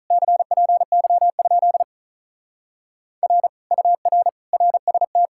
Morse code, heard as a keyed tone, about 44 wpm, 700 Hz. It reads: CPY? R UR RST